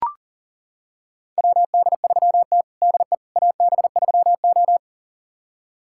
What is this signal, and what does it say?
Morse code, heard as a keyed tone, about 40 wpm, 700 Hz. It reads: WD3T DE AB3O
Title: USA Callsign DE Callsign - 4 Characters - 40wpm